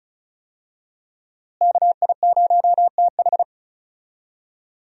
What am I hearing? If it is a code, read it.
Morse code, heard as a keyed tone, about 36 wpm, 700 Hz. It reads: KI0TH
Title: USA Call Signs - 5 Characters - Repeated 3x - 35wpm